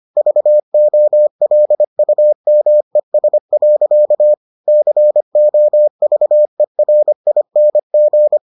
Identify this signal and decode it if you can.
Morse code, heard as a keyed tone, about 25 wpm, 600 Hz. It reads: VOLUMES. COVERING